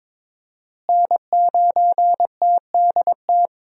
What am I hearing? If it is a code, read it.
Morse code, heard as a keyed tone, about 22 wpm, 700 Hz. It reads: N9TDT
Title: USA Call Signs - 5 Characters - Repeated 3x - 22wpm